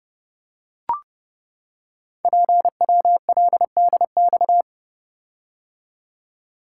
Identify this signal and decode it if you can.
Morse code, heard as a keyed tone, about 30 wpm, 700 Hz. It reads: PWLDX
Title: Code Groups - 5 Letter:Numbers - 30wpm